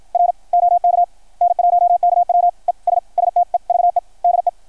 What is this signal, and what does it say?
Morse code, heard as a keyed tone, about 53 wpm, 660 Hz. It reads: R OK N1KW ES STEVE DE
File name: archived / CRSnet